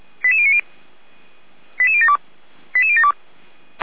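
Five-tone selective call (ZVEI), heard as quick runs of five tones, tones of about 70 ms. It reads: 88090 88072 88072
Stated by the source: ZVEI five-tone selective call received on an SDR, from Feuerwehr Orlamünde fire brigade.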